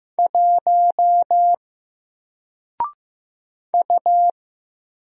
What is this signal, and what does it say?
Morse code, heard as a keyed tone, about 15 wpm, 700 Hz. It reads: 1 U
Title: Single Letter-Number 15wpm